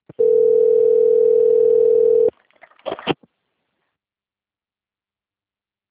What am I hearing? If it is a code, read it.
Ringback tone (North America), 440 + 480 Hz.
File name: out- - - - - .17302